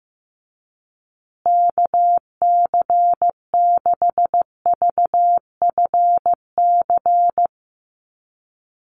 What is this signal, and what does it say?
Morse code, heard as a keyed tone, about 14 wpm, 700 Hz. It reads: KC6VFC
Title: USA Call Signs Repeated 2x 15wpm